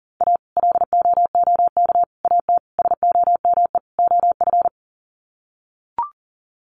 Morse code, heard as a keyed tone, about 37 wpm, 700 Hz. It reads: A LOOK AT SOME OF
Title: Sets of 5 Words Encore2 40wpm